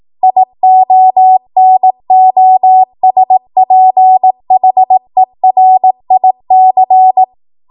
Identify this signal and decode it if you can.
Morse code, heard as a keyed tone, about 18 wpm, 750 Hz. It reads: IONOSPHERIC
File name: Morse / 18WPM